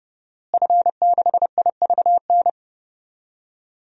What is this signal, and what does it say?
Morse code, heard as a keyed tone, about 30 wpm, 700 Hz. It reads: F6IVN
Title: DXCC-100 Call Signs - 5 Characters - Repeated 3x - 30wpm